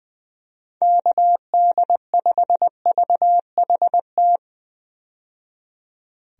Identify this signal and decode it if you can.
Morse code, heard as a keyed tone, about 21 wpm, 700 Hz. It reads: KD5VHT
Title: USA Call Signs - 6 Characters - 20wpm